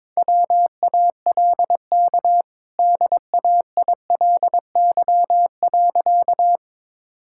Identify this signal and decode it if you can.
Morse code, heard as a keyed tone, about 22 wpm, 700 Hz. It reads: WALK DAILY.
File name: practice / 22wpm / Easy Sentences with Hints - 22wpm